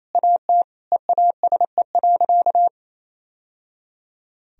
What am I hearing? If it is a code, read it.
Morse code, heard as a keyed tone, about 28 wpm, 700 Hz. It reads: AT EASE.
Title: Sentences from Top 1000 Words - Review Part 1 - 28wpm